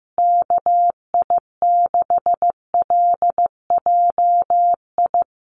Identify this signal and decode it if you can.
Morse code, heard as a keyed tone, about 15 wpm, 700 Hz. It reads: KI6LJI
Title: USA Call Signs Repeated 2x 15wpm